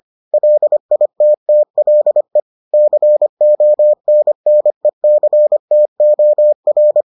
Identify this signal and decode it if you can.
Morse code, heard as a keyed tone, about 25 wpm, 600 Hz. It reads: LITTLE CONNECTOR